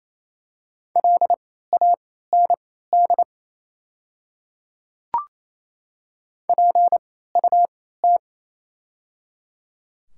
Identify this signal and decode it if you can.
Morse code, heard as a keyed tone, about 28 wpm, spaced as for 16 wpm, 700 Hz. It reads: LAND PUT
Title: 3x Character Spacing - Top 300 Words - 28wpm